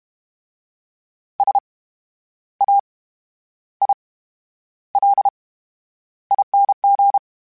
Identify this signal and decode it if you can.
Morse code, heard as a keyed tone, about 32 wpm, spaced as for 11 wpm, 800 Hz. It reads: SAILING
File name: Special INGs / Special INGs-mp3s / VerySlowToFaster Ing @30